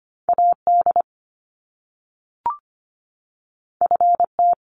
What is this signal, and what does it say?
Morse code, heard as a keyed tone, about 25 wpm, 700 Hz. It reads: AD FT
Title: Top Two Letter Combinations in Words 25wpm